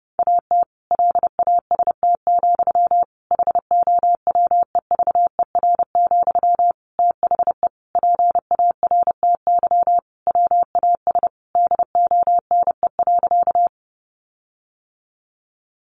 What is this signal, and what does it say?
Morse code, heard as a keyed tone, about 28 wpm, 700 Hz. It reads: AT LAST, HOWEVER, THE PARTY WAS DONE.
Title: Sentences from Top 1400 Words 30wpm